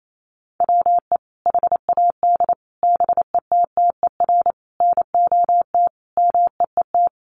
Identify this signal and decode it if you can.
Morse code, heard as a keyed tone, about 28 wpm, 700 Hz. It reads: WE HAD BETTER NOT MEET
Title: Sentences from Top 1100 Words 28wpm